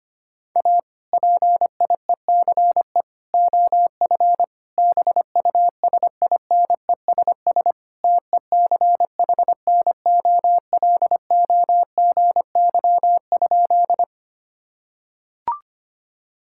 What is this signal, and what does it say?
Morse code, heard as a keyed tone, about 25 wpm, 700 Hz. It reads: A PIECE OF BUSINESS TECHNOLOGY?
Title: Sentences from Top 1000 Words - Review Part 5 - 25wpm